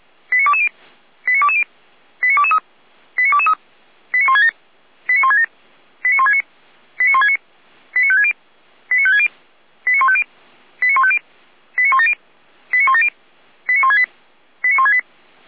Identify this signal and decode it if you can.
Five-tone selective call (ZVEI), heard as quick runs of five tones, tones of about 70 ms. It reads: 89220 89220 89223 89223 89167 89167 89169 89169 89560 89560 89150 89150 89170 89170 89168 89168